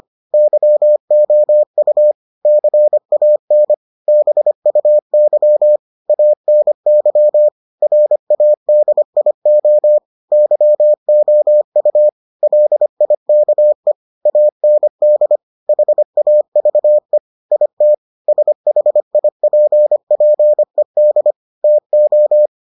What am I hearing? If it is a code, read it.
Morse code, heard as a keyed tone, about 25 wpm, 600 Hz. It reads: YOU CAN BUY ANY RADIO YOU LIKE AND HAVE IT SHIPPED TO